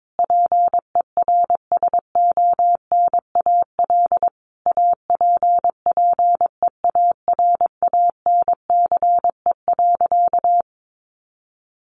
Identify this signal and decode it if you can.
Morse code, heard as a keyed tone, about 22 wpm, 700 Hz. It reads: PERSONAL APPEARANCE.
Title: Sentences from Top 1100 Words 22wpm